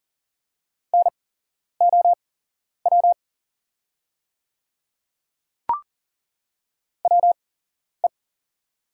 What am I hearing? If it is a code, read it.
Morse code, heard as a keyed tone, about 40 wpm, spaced as for 11 wpm, 690 Hz. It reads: NOW WE
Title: 8x Character Spacing - Top 100 Words - 40wpm